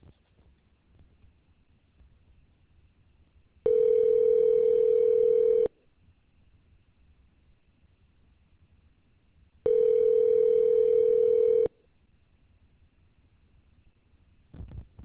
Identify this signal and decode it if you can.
Ringback tone (North America), 440 + 480 Hz.